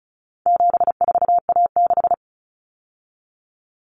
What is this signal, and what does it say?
Morse code, heard as a keyed tone, about 34 wpm, 700 Hz. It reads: 74A6
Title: Common Cut Numbers - 4-Digit 35wpm